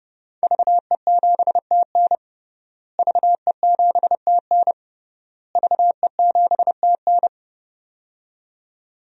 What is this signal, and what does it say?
Morse code, heard as a keyed tone, about 30 wpm, 700 Hz. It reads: VE7TN VE7TN VE7TN
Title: DXCC-100 Call Signs - 5 Characters - Repeated 3x - 30wpm